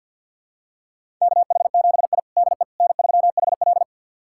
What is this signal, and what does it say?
Morse code, heard as a keyed tone, about 52 wpm, 700 Hz. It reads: KH7I DE N3HL
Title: USA Callsign DE Callsign - 4 Characters - 50wpm